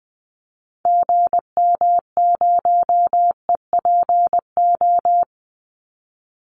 Morse code, heard as a keyed tone, about 20 wpm, 700 Hz. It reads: GM0EPO
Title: Call Signs - 20wpm